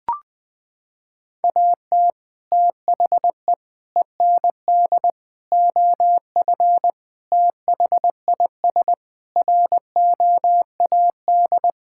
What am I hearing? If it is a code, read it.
Morse code, heard as a keyed tone, about 21 wpm, 700 Hz. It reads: AT THE END OF THIS ROAD
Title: Sentences with 5-Word Phrases - 20wpm